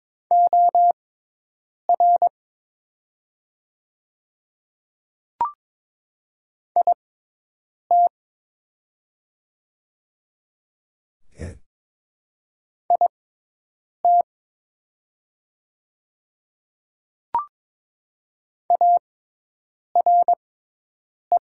Morse code, heard as a keyed tone, about 22 wpm, spaced as for 8 wpm, 700 Hz. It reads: OR IT IT ARE
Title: 6x Character Spacing - Top 100 Words - 22wpm